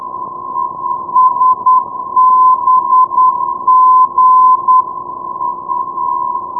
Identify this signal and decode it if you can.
Morse code, heard as a keyed tone, about 10 wpm, 1020 Hz. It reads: SBG
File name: europe / aut / aut-40 ndb salzburg sbg 382.0 (+ fw-382) dkb 2009